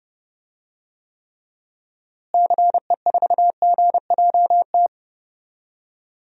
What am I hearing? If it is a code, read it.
Morse code, heard as a keyed tone, about 29 wpm, 700 Hz. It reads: CE4GJT
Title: DXCC-100 Call Signs - 6 Characters - Repeated 3x - 30wpm